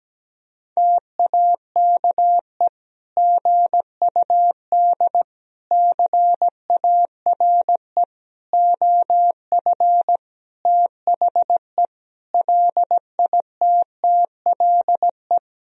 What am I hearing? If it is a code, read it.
Morse code, heard as a keyed tone, about 17 wpm, 700 Hz. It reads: TAKE GUD CARE OF THE LITTLE
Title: Sentences from Top 400 Words and Abbreviations 17wpm